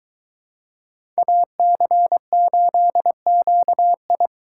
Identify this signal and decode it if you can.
Morse code, heard as a keyed tone, about 23 wpm, 700 Hz. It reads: AC8QI